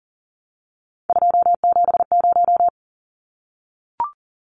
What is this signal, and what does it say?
Morse code, heard as a keyed tone, about 40 wpm, 700 Hz. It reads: 270
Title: Numbers - 3-Digit 40wpm